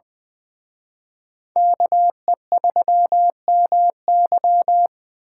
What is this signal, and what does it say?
Morse code, heard as a keyed tone, about 25 wpm, 700 Hz. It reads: KE3MY
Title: Speed Racing - USA 2x2 Call Signs 20wpm